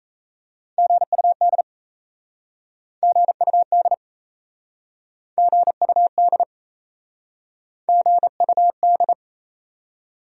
Morse code, heard as a keyed tone, about 38 wpm, 700 Hz. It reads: GUD GUD GUD GUD